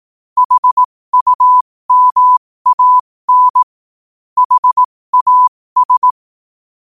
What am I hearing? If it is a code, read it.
Morse code, heard as a keyed tone, about 18 wpm, spaced as for 15 wpm, 1000 Hz. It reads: HUMAN HAS
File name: qotdfiles / 15wpm